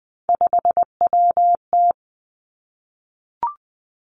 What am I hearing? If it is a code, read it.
Morse code, heard as a keyed tone, about 20 wpm, 700 Hz. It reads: 5WT
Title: Three Challenging Letter-Number Combinations 20wpm